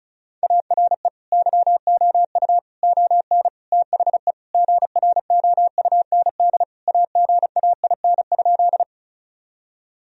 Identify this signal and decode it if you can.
Morse code, heard as a keyed tone, about 36 wpm, 700 Hz. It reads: ARE YOU ON THE GROUND AGAIN?
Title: Sentences from Top 1000 Words - Review Part 2 - 35wpm